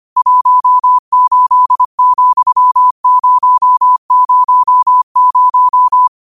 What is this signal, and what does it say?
Morse code, heard as a keyed tone, about 25 wpm, 1000 Hz. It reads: 18,000
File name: qotdfiles / 25wpm